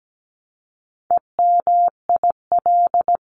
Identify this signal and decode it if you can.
Morse code, heard as a keyed tone, about 17 wpm, 700 Hz. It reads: EMIL
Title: CWT Exchange 17wpm